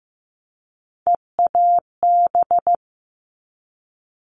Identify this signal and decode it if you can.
Morse code, heard as a keyed tone, about 15 wpm, 700 Hz. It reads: EAB
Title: All Cut Numbers - 3-Digit 15wpm